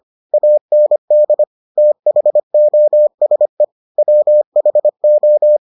Morse code, heard as a keyed tone, about 25 wpm, 600 Hz. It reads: AND THOSE WHO